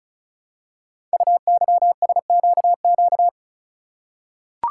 Morse code, heard as a keyed tone, about 36 wpm, 690 Hz. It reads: UYSQQ
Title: Code Groups - 5 Letters - 35wpm